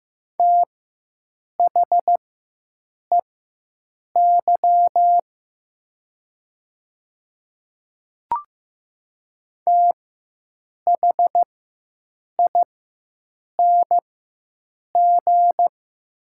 Morse code, heard as a keyed tone, about 15 wpm, spaced as for 7 wpm, 700 Hz. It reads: THEY THING
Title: 4x Character Spacing - Top 100 Words - 15wpm